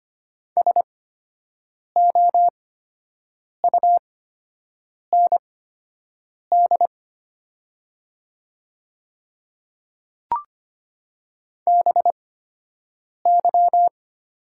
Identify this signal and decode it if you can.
Morse code, heard as a keyed tone, about 25 wpm, spaced as for 7 wpm, 700 Hz. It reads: SOUND BY
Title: 8x Character Spacing - Top 100 Words - 25wpm